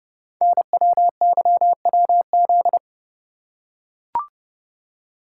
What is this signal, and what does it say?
Morse code, heard as a keyed tone, about 30 wpm, 700 Hz. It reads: NWYWZ
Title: Code Groups - 5 Letters - 30wpm